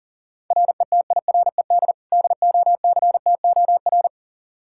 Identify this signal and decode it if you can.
Morse code, heard as a keyed tone, about 37 wpm, 700 Hz. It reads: RETIRED DOCTOR
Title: Practice QSO - Exchange 3 - More Info - 40wpm